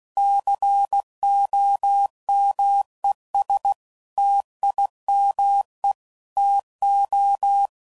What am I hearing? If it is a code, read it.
Morse code, heard as a keyed tone, about 16 wpm, 800 Hz. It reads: COMES TIME TO
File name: CW Code Groups @15WPM / Technical Subjects / AND Gate